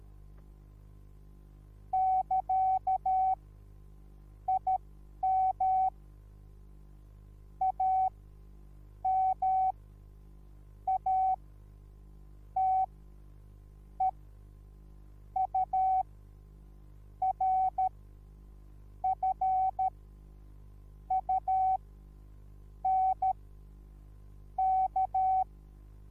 Morse code, heard as a keyed tone, about 13 wpm, spaced as for 6 wpm, 760 Hz. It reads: <KA> IM AMATEURFUNK
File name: Onlinelehrgang / morsen / DJ4UF Morsen21